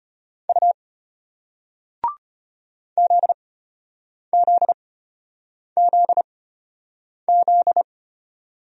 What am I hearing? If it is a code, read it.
Morse code, heard as a keyed tone, about 34 wpm, spaced as for 8 wpm, 700 Hz. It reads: U ZZZZ